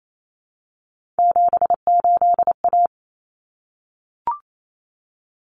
Morse code, heard as a keyed tone, about 28 wpm, 700 Hz. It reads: 78A
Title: Common Cut Numbers - 3-Digit 28wpm